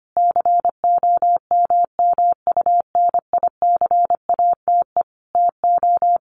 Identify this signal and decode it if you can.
Morse code, heard as a keyed tone, about 25 wpm, 700 Hz. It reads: COMMUNICATE TO
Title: Sentences from Top 1300 Words 25wpm